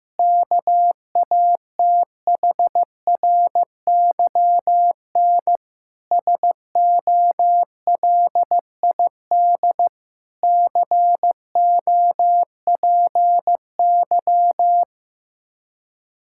Morse code, heard as a keyed tone, about 15 wpm, 700 Hz. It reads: KATHRYN SOLID COPY
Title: K1USN SST QSO Practice 15wpm